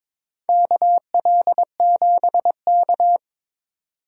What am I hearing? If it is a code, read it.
Morse code, heard as a keyed tone, about 23 wpm, 700 Hz. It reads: KL7K
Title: DXCC-100 Call Signs - 4 Characters - 22wpm